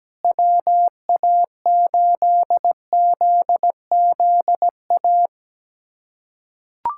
Morse code, heard as a keyed tone, about 17 wpm, 700 Hz. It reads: WA8ZZA
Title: USA Callsign DE Callsign - 6 Characters - 17wpm